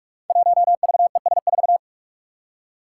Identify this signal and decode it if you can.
Morse code, heard as a keyed tone, about 44 wpm, 700 Hz. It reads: 1VES4
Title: Five Challenging Letter-Number Combinations 45wpm